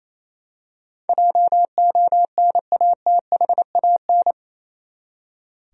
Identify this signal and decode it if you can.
Morse code, heard as a keyed tone, about 28 wpm, 700 Hz. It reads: JONATHAN
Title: USA Common Names - 28wpm